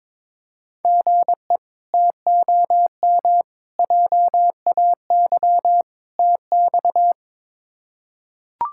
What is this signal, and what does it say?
Morse code, heard as a keyed tone, about 22 wpm, 700 Hz. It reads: GE TOM JAY TX
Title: K1USN SST QSO Practice - 22wpm